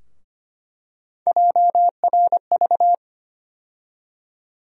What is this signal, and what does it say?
Morse code, heard as a keyed tone, about 25 wpm, 700 Hz. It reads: JRV
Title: USA Call Sign Postfixes - 3 Letters 25wpm